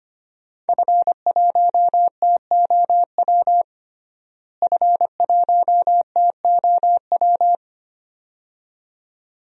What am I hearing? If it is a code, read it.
Morse code, heard as a keyed tone, about 25 wpm, 700 Hz. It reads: F1TOW F1TOW
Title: DXCC-100 Call Signs - 5 Characters - Repeated 2x - 25wpm